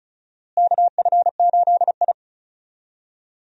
Morse code, heard as a keyed tone, about 36 wpm, 700 Hz. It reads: KF8I
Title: USA Call Signs - 4 Characters - Repeated 2x - 35wpm